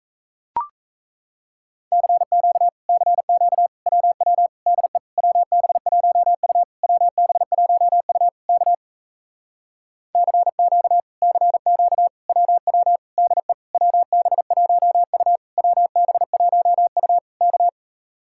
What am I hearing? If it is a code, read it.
Morse code, heard as a keyed tone, about 38 wpm, 700 Hz. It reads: CQ CQ WW DE WB1U WB1U K CQ CQ WW DE WB1U WB1U K